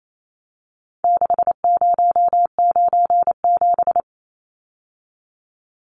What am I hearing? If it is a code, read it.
Morse code, heard as a keyed tone, about 28 wpm, 700 Hz. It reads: 6097
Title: Numbers - 4-Digit 28wpm